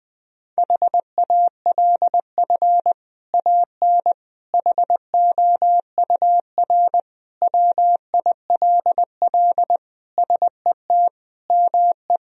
Morse code, heard as a keyed tone, about 20 wpm, 700 Hz. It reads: HALF AN HOUR WILL SET ME
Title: Sentences from Top 1000 Words - Review Part 1 - 20wpm